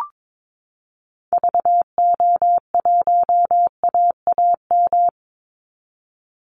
Morse code, heard as a keyed tone, about 22 wpm, 700 Hz. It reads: VO1AAM
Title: Call Signs Repeated 3x - 22wpm